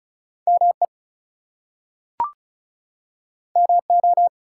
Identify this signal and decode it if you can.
Morse code, heard as a keyed tone, about 36 wpm, 700 Hz. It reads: ME MO
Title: US States and Canadian Provinces 35wpm